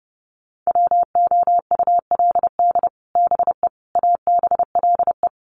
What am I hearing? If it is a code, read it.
Morse code, heard as a keyed tone, about 30 wpm, 700 Hz. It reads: WOULD BE ABLE
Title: Sets of 4 Words 30wpm